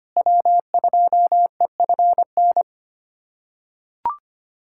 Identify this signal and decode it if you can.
Morse code, heard as a keyed tone, about 25 wpm, 700 Hz. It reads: W2EFN
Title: USA Callsign DE Callsign - 5 Characters - 25wpm